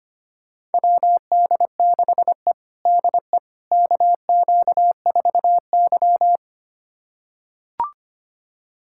Morse code, heard as a keyed tone, about 25 wpm, 700 Hz. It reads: WD6E DE KQ4Y
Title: USA Callsign DE Callsign - 4 Characters - 25wpm